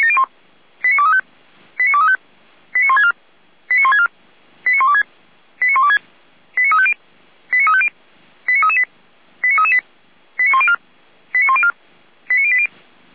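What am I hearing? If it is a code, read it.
Five-tone selective call (ZVEI), heard as quick runs of five tones, tones of about 70 ms. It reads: 89235 89235 89164 89164 89126 89126 89350 89350 89338 89338 89104 89104 89080
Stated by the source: ZVEI five-tone selective call received with an SDR.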